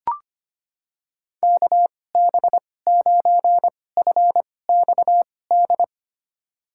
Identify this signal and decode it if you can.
Morse code, heard as a keyed tone, about 25 wpm, spaced as for 18 wpm, 700 Hz. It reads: KB9FXD
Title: USA Call Signs - 6 Characters - 2x Character Spacing - 25wpm